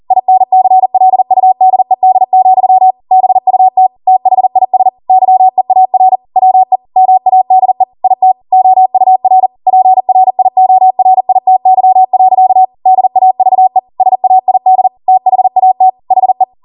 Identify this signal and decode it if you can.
Morse code, heard as a keyed tone, about 40 wpm, 750 Hz. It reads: INCLUDED, BUT THIS YEAR WE MADE IT OUR PRIORITY. DAVE SAID THAT HE